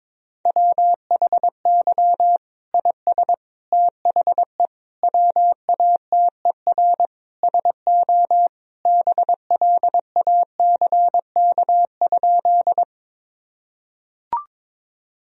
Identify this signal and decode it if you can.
Morse code, heard as a keyed tone, about 22 wpm, 700 Hz. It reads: WHY IS THE WATER SO BLACK?
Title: Sentences from Top 1000 Words - Review Part 2 - 22wpm